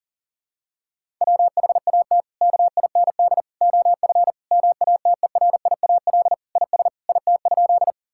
Morse code, heard as a keyed tone, about 37 wpm, 700 Hz. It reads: WHAT KIND OF MATERIAL IS IT?
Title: Sentences from Top 1000 Words - Review Part 5 - 40wpm